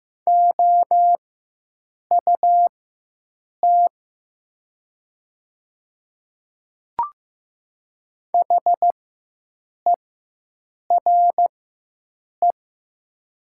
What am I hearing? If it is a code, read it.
Morse code, heard as a keyed tone, about 15 wpm, spaced as for 7 wpm, 700 Hz. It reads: OUT HERE
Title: 4x Character Spacing - Top 300 Words - 15wpm